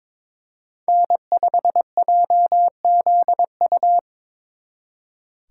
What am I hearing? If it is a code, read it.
Morse code, heard as a keyed tone, about 25 wpm, 700 Hz. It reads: N5JZU
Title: Speed Racing - USA 1x3 Call Signs 22wpm